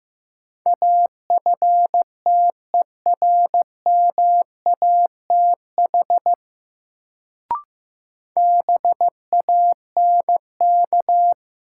Morse code, heard as a keyed tone, about 15 wpm, 700 Hz. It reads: AFTERMATH BANK
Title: Sets of 5 Words - New Word Drill - 15wpm